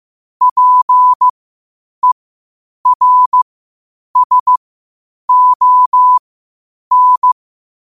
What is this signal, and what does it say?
Morse code, heard as a keyed tone, about 15 wpm, spaced as for 8 wpm, 1000 Hz. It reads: PERSON
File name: qotdfiles / 8wpm